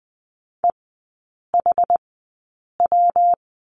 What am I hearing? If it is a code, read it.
Morse code, heard as a keyed tone, about 20 wpm, spaced as for 8 wpm, 700 Hz. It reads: EHW
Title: 114 Lesson 19e - Letter w - Three Characters 20wpm